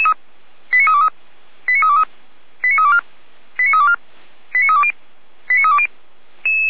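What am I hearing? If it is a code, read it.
Five-tone selective call (ZVEI), heard as quick runs of five tones, tones of about 70 ms. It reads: 89323 89323 89324 89324 89320 89320
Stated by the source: ZVEI five-tone call received by SDR, feuerwehr-orlamuende.org fire brigade server.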